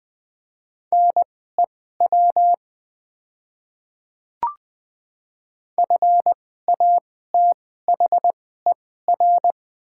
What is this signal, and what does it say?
Morse code, heard as a keyed tone, about 20 wpm, spaced as for 15 wpm, 700 Hz. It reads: NEW FATHER
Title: 2x Character Spacing - Top 200 Words - 20wpm